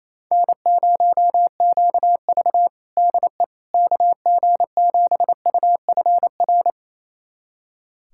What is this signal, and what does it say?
Morse code, heard as a keyed tone, about 28 wpm, 700 Hz. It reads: N0QV DE KG7UFR
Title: USA Callsign DE Callsign 28wpm